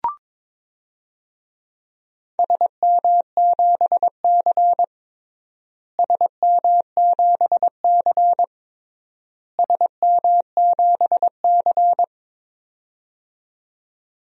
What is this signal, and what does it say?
Morse code, heard as a keyed tone, about 22 wpm, 700 Hz. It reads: SM7C SM7C SM7C
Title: DXCC-100 Call Signs - 4 Characters - Repeated 3x - 22wpm